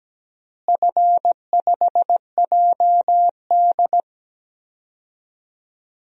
Morse code, heard as a keyed tone, about 17 wpm, 700 Hz. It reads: F5JD
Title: DXCC-100 Call Signs - 4 Characters - Repeated 3x - 17wpm